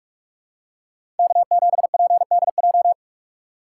Morse code, heard as a keyed tone, about 44 wpm, 700 Hz. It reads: K7PDJ
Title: USA Call Signs - 5 Characters - 6x Character Spacing - 45wpm